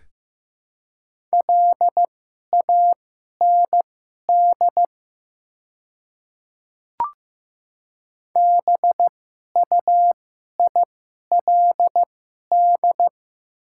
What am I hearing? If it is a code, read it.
Morse code, heard as a keyed tone, about 15 wpm, spaced as for 11 wpm, 700 Hz. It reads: LAND BUILD
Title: 2x Character Spacing - Top 300 Words - 15wpm